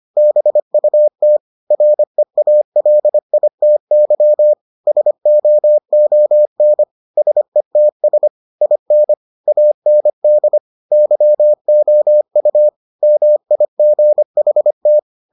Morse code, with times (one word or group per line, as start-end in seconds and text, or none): 0.2-1.4 BUT
1.7-4.5 REALITY
4.9-6.8 SOON
7.2-8.3 SETS
8.6-9.1 IN
9.5-10.6 AND
10.9-12.7 YOU
13.0-15.0 MIGHT